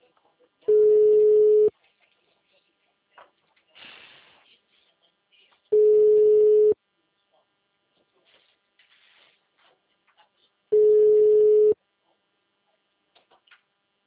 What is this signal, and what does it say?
Ringback tone (Europe, 425 Hz), with speech.